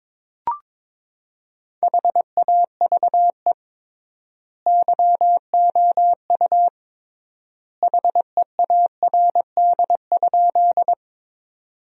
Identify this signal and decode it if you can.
Morse code, heard as a keyed tone, about 22 wpm, 700 Hz. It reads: HAVE YOU HEARD?
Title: 3x Word-Spacing - Sentences from Top 400 Words - 22wpm